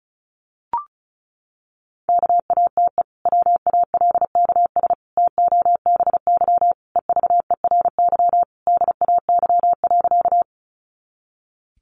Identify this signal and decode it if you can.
Morse code, heard as a keyed tone, about 34 wpm, 700 Hz. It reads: KATE WALKS TOBY EVERY DAY.
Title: Easy Phrases 35wpm